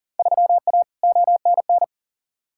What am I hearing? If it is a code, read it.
Morse code, heard as a keyed tone, about 40 wpm, 700 Hz. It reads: 3A ONN
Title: Field Day Exchanges Encore - 40wpm